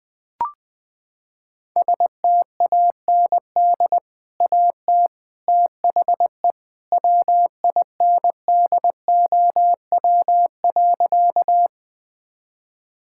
Morse code, heard as a keyed tone, about 20 wpm, 700 Hz. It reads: STAND AT THE WINDOW.
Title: Sentences from Top 1600 Words 20wpm